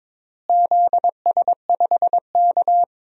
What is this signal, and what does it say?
Morse code, heard as a keyed tone, about 22 wpm, 700 Hz. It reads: ZS5K
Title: DXCC-100 Call Signs - 4 Characters - Repeated 3x - 22wpm